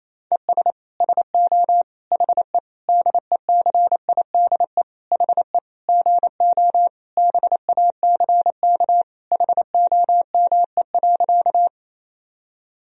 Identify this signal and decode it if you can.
Morse code, heard as a keyed tone, about 28 wpm, 700 Hz. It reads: ES SO HE DECIDE HE GO BACK HOME.
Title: Sentences from Top 500 Words and Abbreviations 28wpm